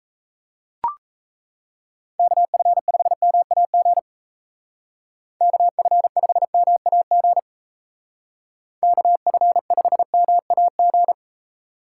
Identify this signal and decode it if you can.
Morse code, heard as a keyed tone, about 37 wpm, 700 Hz. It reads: KF5MAG KF5MAG KF5MAG